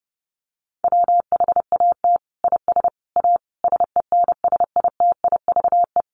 Morse code, heard as a keyed tone, about 30 wpm, 700 Hz. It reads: WHAT IS A SENSITIVE
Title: Sentences from Top 1400 Words 30wpm